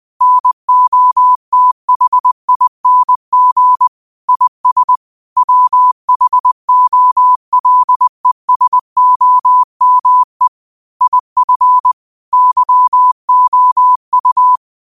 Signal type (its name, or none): Morse code